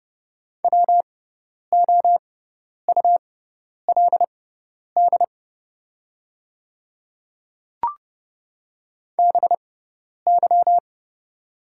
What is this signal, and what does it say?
Morse code, heard as a keyed tone, about 30 wpm, spaced as for 10 wpm, 700 Hz. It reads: WOULD BY
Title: 6x Character Spacing - Top 100 Words - 30wpm